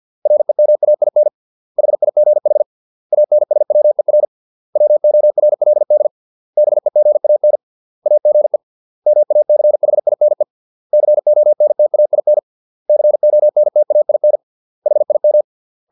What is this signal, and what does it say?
Morse code, heard as a keyed tone, about 50 wpm, 600 Hz. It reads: REMAIN HIGH ANSWER WORLD BEGAN AGE MACHINE CONTAIN CONTAIN HIM